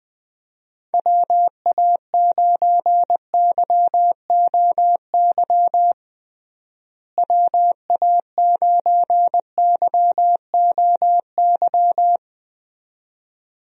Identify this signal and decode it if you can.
Morse code, heard as a keyed tone, about 20 wpm, 700 Hz. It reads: WA9YOY WA9YOY
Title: USA Call Signs - 6 Characters - Repeated 2x - 20wpm